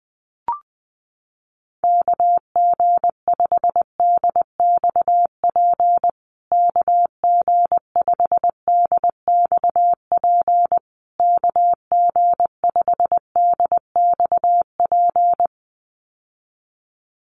Morse code, heard as a keyed tone, about 20 wpm, 700 Hz. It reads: KG5DXP KG5DXP KG5DXP
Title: USA Call Signs Repeated 3x 20wpm